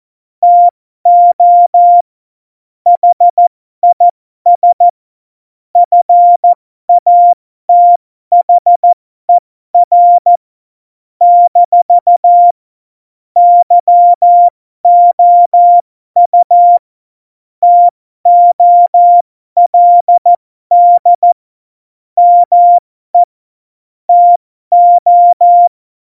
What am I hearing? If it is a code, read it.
Morse code, heard as a keyed tone, about 14 wpm, spaced as for 12 wpm, 700 Hz. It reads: TO HIS FATHER - YOU TOLD ME TO
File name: Morsekurs / Witze